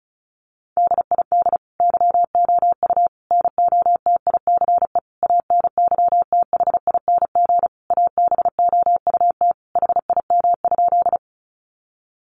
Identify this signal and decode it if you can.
Morse code, heard as a keyed tone, about 34 wpm, 700 Hz. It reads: DID YOU NOTICE ANYTHING ABOUT HIM?
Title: Sentences from Top 1000 Words 35wpm